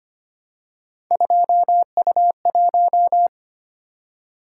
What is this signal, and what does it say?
Morse code, heard as a keyed tone, about 25 wpm, 700 Hz. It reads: 2U1
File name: Three Letter-Number Combinations - 25wpm